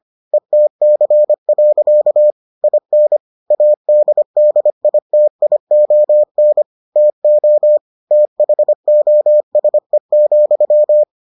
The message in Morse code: ETC. IN ADDITION TO THOSE,

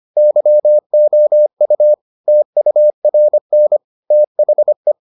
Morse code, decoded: YOU TURN THE

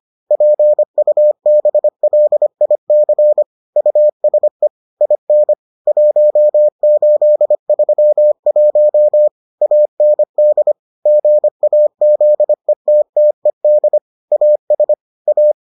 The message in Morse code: PUBLIC USE IN 1831 AND GAZETTED AS A

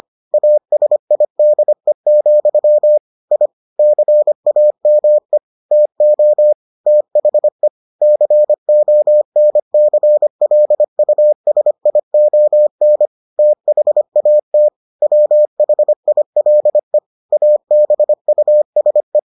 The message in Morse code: ASIDE, I CAME TO THE CONCLUSION THAT WHILE ABUSE

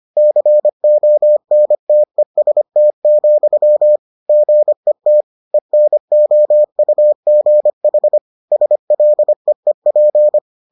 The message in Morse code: CONTEST, GET ENOUGH SLEEP

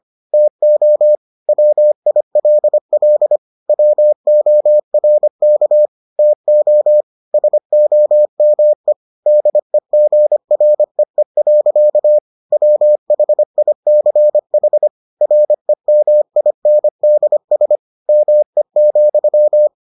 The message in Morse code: TO WILL WORK TO SOME DEGREE. WHICH REMINDS ME,

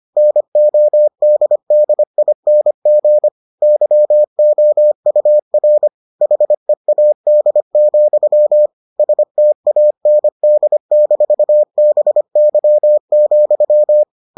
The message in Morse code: NODDING YOUR HEAD, STAND-BY,

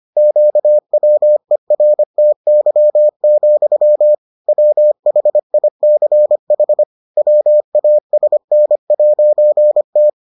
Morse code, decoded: QWERTY, WHICH WASN'T

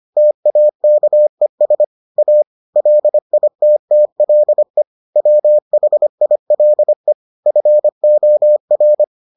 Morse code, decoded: TAKES A LITTLE WHILE FOR

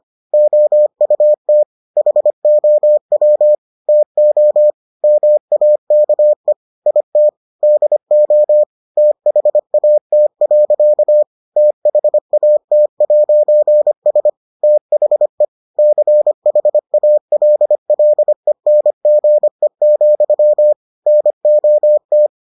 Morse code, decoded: OUT HOW TO MAKE IT DO THAT. THAT'S THE CHALLENGE, NOT

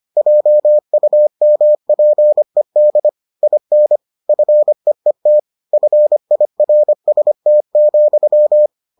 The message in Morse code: JUMPED IN FEET FIRST,

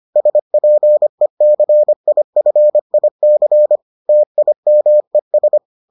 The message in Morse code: SPECIFIC TIMES

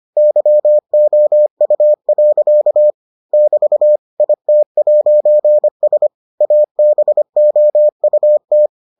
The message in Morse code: YOU. <BT> IT'S ABOUT

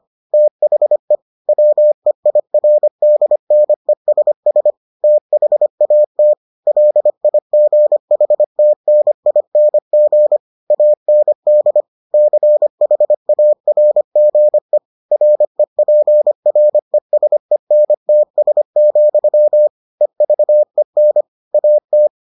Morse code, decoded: THE WEIRDNESS THAT LIGHTNING AND CHARGE REPRESENTS, EVEN AT

